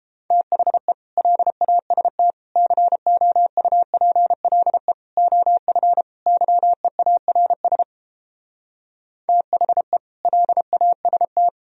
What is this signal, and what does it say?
Morse code, heard as a keyed tone, about 31 wpm, 700 Hz. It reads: THE LAST COUPLE OF YEARS THE LAST